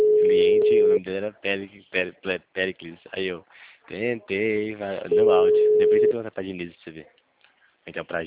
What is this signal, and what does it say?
Ringback tone (Europe, 425 Hz), with speech.
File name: out- -2013- - - .66028